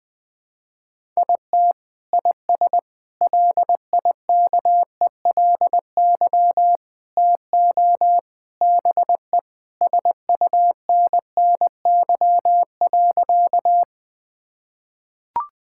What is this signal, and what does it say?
Morse code, heard as a keyed tone, about 20 wpm, 700 Hz. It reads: IT IS LIKELY TO BE SUNNY.
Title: Sentences with 3-Word Phrases - Spoken Beginning - 20wpm